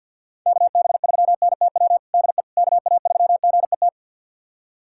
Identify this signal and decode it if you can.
Morse code, heard as a keyed tone, about 51 wpm, 700 Hz. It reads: KB3NTW DE KA3GET
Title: USA Callsign DE Callsign - 6 Characters - 50wpm